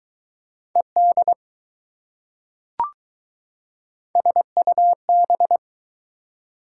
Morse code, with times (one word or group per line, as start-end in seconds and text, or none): 0.8-1.3 ED
4.2-5.6 SUB